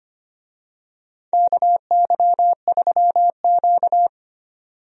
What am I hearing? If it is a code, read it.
Morse code, heard as a keyed tone, about 25 wpm, 700 Hz. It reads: KY3Q
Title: USA Call Signs - 4 Characters - 25wpm